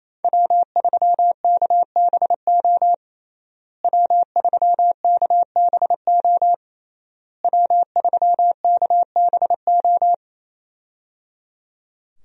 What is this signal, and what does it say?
Morse code, heard as a keyed tone, about 28 wpm, 700 Hz. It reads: W3KBO W3KBO W3KBO
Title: USA Call Signs - 5 Characters - Repeated 3x - 28wpm